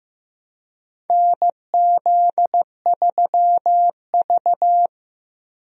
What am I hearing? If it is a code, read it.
Morse code, heard as a keyed tone, about 15 wpm, 700 Hz. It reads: NZ3V
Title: USA Call Signs - 4 Characters - Repeated 3x - 15wpm